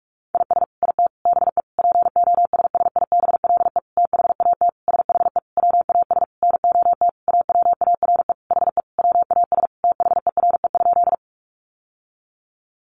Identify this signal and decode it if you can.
Morse code, heard as a keyed tone, about 44 wpm, 700 Hz. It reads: IS IT BE POSSIBLE THAT SHE WAS NOT AWARE HE WAS THERE?
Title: Sentences from Top 1200 Words 45wpm